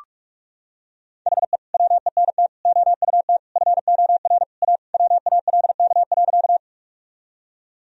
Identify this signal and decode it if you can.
Morse code, heard as a keyed tone, about 46 wpm, 700 Hz. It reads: HE WENT OUT FOR A WALK.
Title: Sentences from Top 1000 Words - Review Part 1 - 45wpm